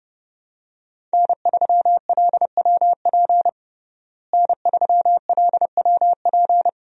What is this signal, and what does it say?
Morse code, heard as a keyed tone, about 30 wpm, 700 Hz. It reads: N3LWP N3LWP
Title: USA Call Signs - 5 Characters - Repeated 2x - 30wpm